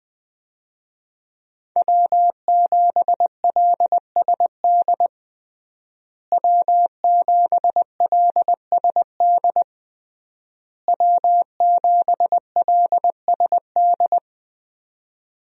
Morse code, heard as a keyed tone, about 20 wpm, 700 Hz. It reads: W7LSD W7LSD W7LSD
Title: USA Call Signs - 5 Characters - Repeated 3x - 20wpm